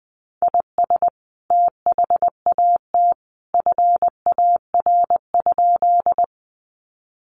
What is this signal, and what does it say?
Morse code, heard as a keyed tone, about 20 wpm, 700 Hz. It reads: IS THAT FAR?
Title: Sentences from Top 300 Words 20wpm